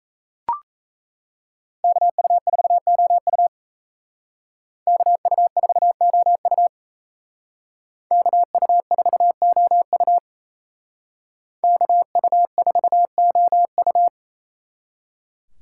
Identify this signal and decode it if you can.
Morse code, heard as a keyed tone, about 37 wpm, 700 Hz. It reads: KU4OU KU4OU KU4OU KU4OU